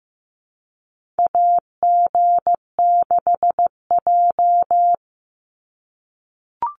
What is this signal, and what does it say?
Morse code, heard as a keyed tone, about 15 wpm, 700 Hz. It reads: AG6J
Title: USA Call Signs 15wpm